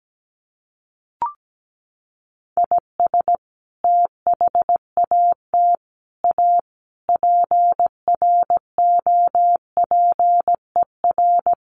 Morse code, heard as a keyed tone, about 17 wpm, 700 Hz. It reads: IS THAT A PROPER
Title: Sentences from Top 1100 Words 17wpm